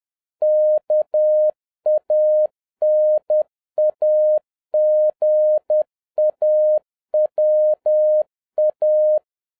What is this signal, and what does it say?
Morse code, heard as a keyed tone, about 10 wpm, 610 Hz. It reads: KANAGAWA
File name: 2011-07-mp3 / n31 THE 3.11 2 10wpm